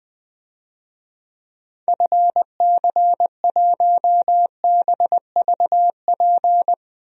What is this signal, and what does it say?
Morse code, heard as a keyed tone, about 21 wpm, 700 Hz. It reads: FC1BVP
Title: DXCC-100 Call Signs - 6 Characters - Repeated 3x - 20wpm